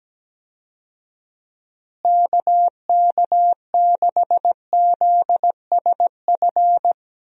Morse code, heard as a keyed tone, about 17 wpm, 700 Hz. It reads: KK6ZSF
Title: USA Call Signs - 6 Characters - Repeated 2x - 17wpm